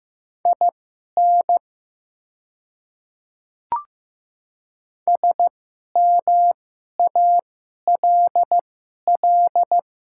Morse code, heard as a keyed tone, about 15 wpm, spaced as for 11 wpm, 700 Hz. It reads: IN SMALL
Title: 2x Character Spacing - Top 200 Words - 15wpm